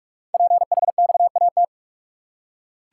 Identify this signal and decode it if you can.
Morse code, heard as a keyed tone, about 44 wpm, 700 Hz. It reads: PHXAT
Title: Code Groups - 5 Letter:Numbers - 45wpm